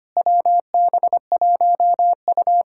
Morse code, heard as a keyed tone, about 25 wpm, 700 Hz. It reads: WB1U
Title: Speed-Racing Common QSO Phrases - 25wpm